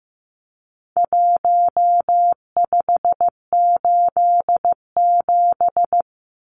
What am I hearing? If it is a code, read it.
Morse code, heard as a keyed tone, about 15 wpm, 700 Hz. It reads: 1587
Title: Numbers - 4-Digit 15wpm